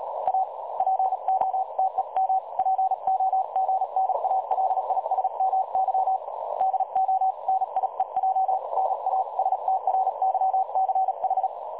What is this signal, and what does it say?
Morse code, heard as a keyed tone, about 36 wpm, 760 Hz, with noise. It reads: A POTEM POGNALISMY DO NIEJ NA STRYCH
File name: dlazaawansowanych / nagraniazpasma / websdr recording 3565.0kHz